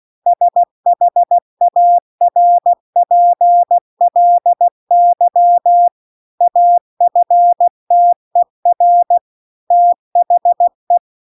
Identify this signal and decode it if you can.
Morse code, heard as a keyed tone, about 16 wpm, 700 Hz. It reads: SHARPLY AFTER THE